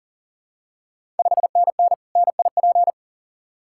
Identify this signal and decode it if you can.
Morse code, heard as a keyed tone, about 38 wpm, 690 Hz. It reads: 5NN NIP
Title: Ontario QSO Party Exchange 40wpm